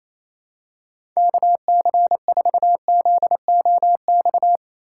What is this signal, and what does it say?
Morse code, heard as a keyed tone, about 28 wpm, 700 Hz. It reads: KC4ZOX
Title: USA Call Signs - 6 Characters - 8x Character Spacing - 28wpm